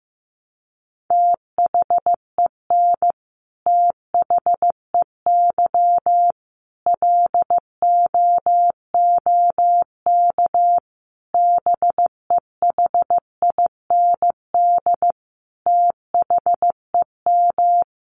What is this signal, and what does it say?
Morse code, heard as a keyed tone, about 15 wpm, 700 Hz. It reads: THEN THEY LOOK BEHIND THEM
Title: Sentences from Top 1300 Words 15wpm